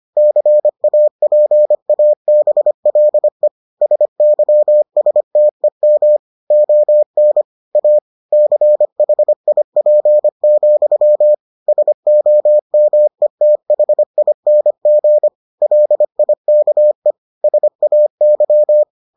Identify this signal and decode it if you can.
Morse code, heard as a keyed tone, about 25 wpm, 600 Hz. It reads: CAPABLE SYSTEM ON A CHIP, SOMETHING LIKE SAY